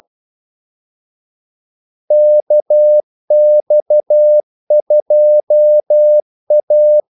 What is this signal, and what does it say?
Morse code, heard as a keyed tone, about 12 wpm, 600 Hz. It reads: KX2A
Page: 2 ekstra ordmellemrum mellem kaldesignalerne